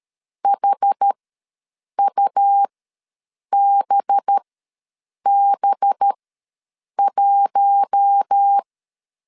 Morse code, heard as a keyed tone, about 13 wpm, spaced as for 7 wpm, 800 Hz. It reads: HUBB1
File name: LEZIONI / 5 LBVC / 5 LB ws 12wpm 3034